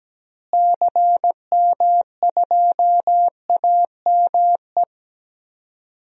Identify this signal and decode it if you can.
Morse code, heard as a keyed tone, about 17 wpm, 700 Hz. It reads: CM2AME
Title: DXCC-100 Call Signs - 6 Characters - Repeated 2x - 17wpm